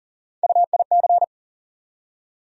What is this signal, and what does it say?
Morse code, heard as a keyed tone, about 40 wpm, 700 Hz. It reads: UIC
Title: USA Call Sign Postfixes - 3 Letters 40wpm